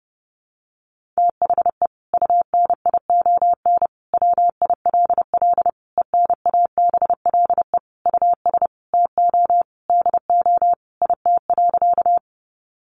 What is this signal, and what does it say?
Morse code, heard as a keyed tone, about 30 wpm, 700 Hz. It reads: THE UNION WILL ENABLE US TO DO IT.
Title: Sentences from Top 1500 Words 30wpm